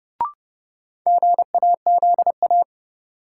Morse code, heard as a keyed tone, about 30 wpm, 700 Hz. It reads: GAZA
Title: Sets of 5 Words - New Word Drill - 30wpm